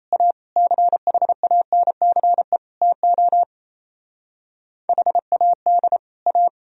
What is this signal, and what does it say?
Morse code, heard as a keyed tone, about 35 wpm, 700 Hz. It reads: A CHANCE TO HAD A